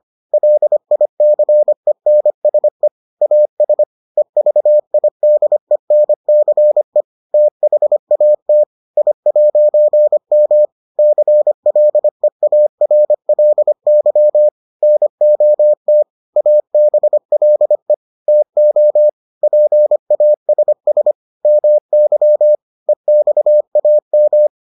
Morse code, heard as a keyed tone, about 25 wpm, 600 Hz. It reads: LICENSE AS EVIDENCE THAT I'M CLEARLY NOT ABLE TO PASS MY EXAM